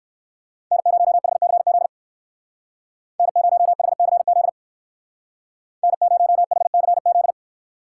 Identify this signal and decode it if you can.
Morse code, heard as a keyed tone, about 59 wpm, 700 Hz. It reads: N0HCZ N0HCZ N0HCZ